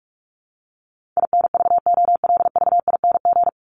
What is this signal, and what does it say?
Morse code, heard as a keyed tone, about 44 wpm, 700 Hz. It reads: INVOLVING
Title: ING Suffix - Top 100 Words with Base Words 45wpm